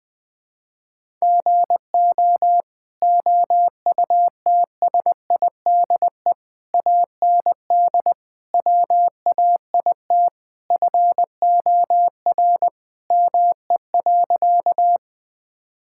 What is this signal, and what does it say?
Morse code, heard as a keyed tone, about 20 wpm, 700 Hz. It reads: GO OUTSIDE AND WAIT FOR ME.
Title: Sentences from Top 1000 Words - Review Part 6 - 20wpm